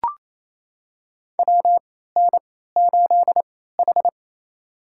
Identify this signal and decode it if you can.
Morse code, heard as a keyed tone, about 28 wpm, spaced as for 16 wpm, 700 Hz. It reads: WN8H
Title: USA Call Signs - 4 Characters - 3x Character Spacing - 28wpm